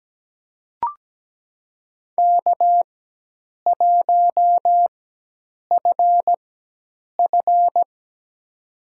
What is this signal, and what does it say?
Morse code, heard as a keyed tone, about 17 wpm, spaced as for 8 wpm, 700 Hz. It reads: K1FF
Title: USA Call Signs - 4 Characters - 4x Character Spacing - 17wpm